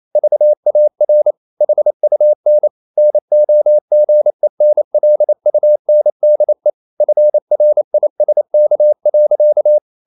Morse code, heard as a keyed tone, about 28 wpm, 600 Hz. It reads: VAR HUN NOGENLUNDE FRISK.
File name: Ferie QRQ 28wpm